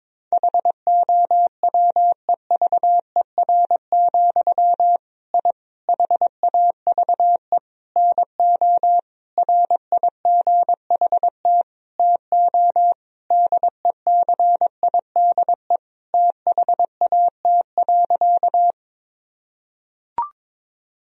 Morse code, heard as a keyed tone, about 22 wpm, 700 Hz. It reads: HOWEVER, I HAVE NO RIGHT TO DECIDE THAT.
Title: Sentences from Top 1000 Words - Review Part 3 - 22wpm